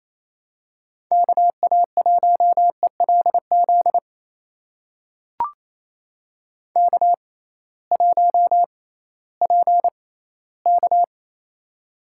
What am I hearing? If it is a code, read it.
Morse code, heard as a keyed tone, about 28 wpm, spaced as for 16 wpm, 700 Hz. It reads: KA1ELZ K1PK
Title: USA Call Signs - 6x Character Spacing - 28wpm